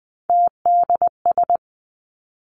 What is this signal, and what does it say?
Morse code, heard as a keyed tone, about 20 wpm, 700 Hz. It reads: TDS
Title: Three Challenging Letter-Number Combinations 20wpm